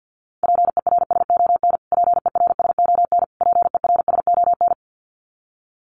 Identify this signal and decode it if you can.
Morse code, heard as a keyed tone, about 50 wpm, 700 Hz. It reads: PERSON PERSON PERSON
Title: Top 500 Words Repeated 50wpm